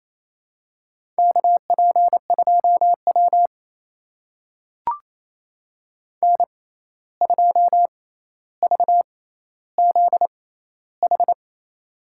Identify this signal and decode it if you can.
Morse code, heard as a keyed tone, about 28 wpm, spaced as for 13 wpm, 700 Hz. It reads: KP2W N2VZH
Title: USA Call Signs - 6x Character Spacing - 28wpm